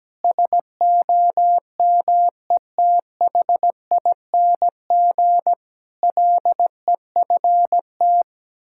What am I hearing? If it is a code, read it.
Morse code, heard as a keyed tone, about 17 wpm, 700 Hz. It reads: SOMETHING LEFT
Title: Sentences from Top 1000 Words - Review Part 6 - 17wpm